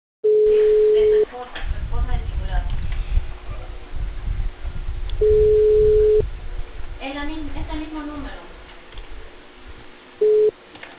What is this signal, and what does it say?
Ringback tone (Europe, 425 Hz), with speech.